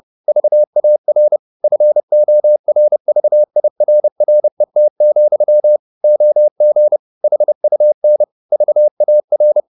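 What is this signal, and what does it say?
Morse code, heard as a keyed tone, about 30 wpm, 600 Hz. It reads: VAR FORVIRRET, OG HUN VAR
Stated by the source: High-speed Morse (QRQ) at 30 wpm.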